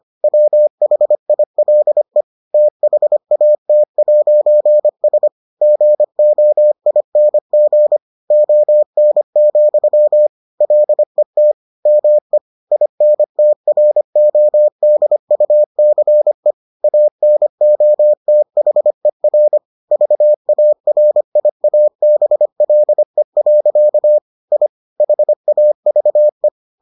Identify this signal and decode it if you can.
Morse code, heard as a keyed tone, about 25 wpm, 600 Hz. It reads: WHILE THAT'S GOING ON, LET ME INTRODUCE ANOTHER VARIABLE. I HAVE